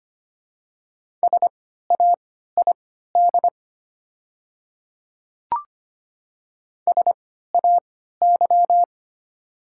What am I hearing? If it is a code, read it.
Morse code, heard as a keyed tone, about 25 wpm, spaced as for 14 wpm, 700 Hz. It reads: SAID SAY
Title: 3x Character Spacing - Top 200 Words - 25wpm